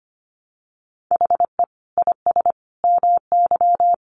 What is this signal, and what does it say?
Morse code, heard as a keyed tone, about 25 wpm, 700 Hz. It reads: HE IS MY
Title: Sentences from Top 300 Words 25wpm